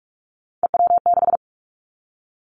Morse code, heard as a keyed tone, about 46 wpm, 700 Hz. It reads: EW6
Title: Callsign Prefixes for Top 50 DXCC - 45wpm